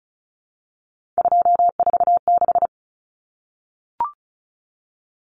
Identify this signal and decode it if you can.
Morse code, heard as a keyed tone, about 34 wpm, 700 Hz. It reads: 246
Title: Numbers - 3-Digit 35wpm